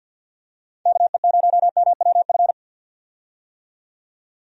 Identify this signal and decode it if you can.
Morse code, heard as a keyed tone, about 46 wpm, 700 Hz. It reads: KE0MWF
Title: USA Call Signs - 6 Characters - 50wpm